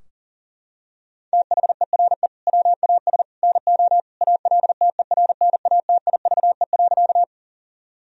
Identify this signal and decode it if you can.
Morse code, heard as a keyed tone, about 40 wpm, 700 Hz. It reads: THERE WAS NO ALTERNATIVE.
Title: Sentences from Top 1000 Words - Review Part 4 - 40wpm